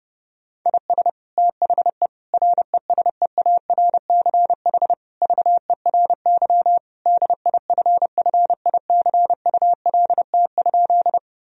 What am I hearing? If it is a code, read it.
Morse code, heard as a keyed tone, about 32 wpm, 700 Hz. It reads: IS THE RESEARCH VERY DIFFICULT?
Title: Sentences from Top 1000 Words - Review Part 4 - 30wpm